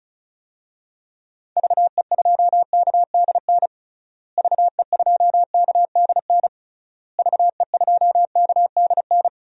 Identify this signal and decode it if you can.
Morse code, heard as a keyed tone, about 36 wpm, 700 Hz. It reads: VE2KDN VE2KDN VE2KDN
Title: DXCC-100 Call Signs - 6 Characters - Repeated 3x - 35wpm